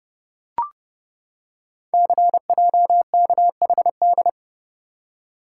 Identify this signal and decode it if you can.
Morse code, heard as a keyed tone, about 30 wpm, 700 Hz. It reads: CJKHD
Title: Code Groups - 5 Letter:Numbers - 30wpm